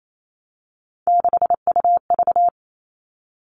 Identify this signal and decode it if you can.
Morse code, heard as a keyed tone, about 28 wpm, 700 Hz. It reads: 6UV
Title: All Cut Numbers - 3-Digit 28wpm